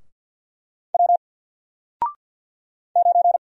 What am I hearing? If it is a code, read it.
Morse code, heard as a keyed tone, about 50 wpm, 700 Hz. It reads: W 9
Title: Single Letter-Number Alt 50wpm